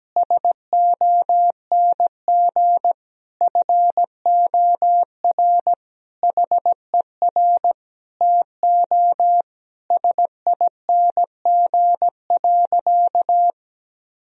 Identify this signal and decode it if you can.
Morse code, heard as a keyed tone, about 17 wpm, 700 Hz. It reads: SONG FOR HER TO SING.
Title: Sentences from Top 1000 Words - Review Part 2 - 17wpm